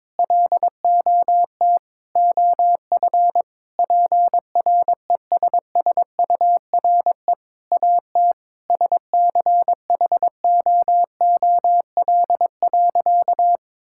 Morse code, heard as a keyed tone, about 22 wpm, 700 Hz. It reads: LOT OF PRESSURE AT SCHOOL.
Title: Sentences from Top 1000 Words - Review Part 6 - 22wpm